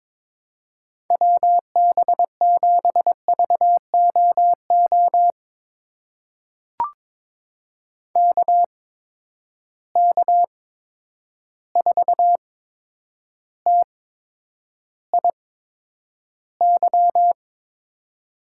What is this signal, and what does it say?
Morse code, heard as a keyed tone, about 23 wpm, spaced as for 10 wpm, 700 Hz. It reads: WB7VOO KK4TIY
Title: USA Call Signs - 6 Characters - 8x Character Spacing - 22wpm